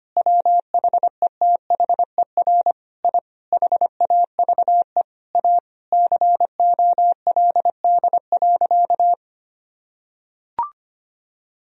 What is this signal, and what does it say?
Morse code, heard as a keyed tone, about 25 wpm, 700 Hz. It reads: WHETHER I HAVE A COLD.
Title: Sentences from Top 1000 Words - Review Part 3 - 25wpm